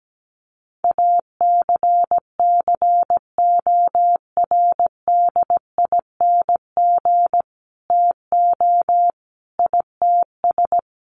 Morse code, heard as a keyed tone, about 17 wpm, 700 Hz. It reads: ACCORDING TO ITS
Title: Sentences from Top 700 Words 17wpm